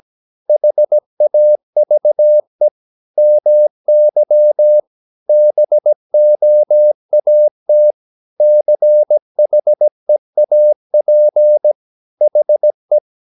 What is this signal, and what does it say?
Morse code, heard as a keyed tone, about 17 wpm, 600 Hz. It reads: HAVE MY BOAT CHEAP HE
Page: Romanen Never af Ken Follet på engelsk.